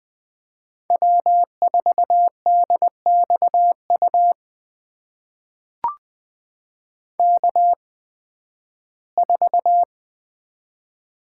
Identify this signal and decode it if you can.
Morse code, heard as a keyed tone, about 20 wpm, spaced as for 13 wpm, 700 Hz. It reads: W4DXU K4
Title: USA Call Signs - 5 Characters - 8x Character Spacing - 20wpm